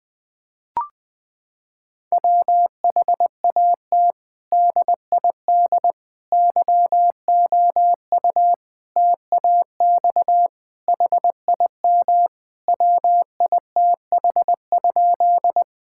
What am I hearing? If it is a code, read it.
Morse code, heard as a keyed tone, about 20 wpm, 700 Hz. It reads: WHAT DID YOU TAX HIM WITH?
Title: Sentences from Top 1000 Words - Review Part 5 - 20wpm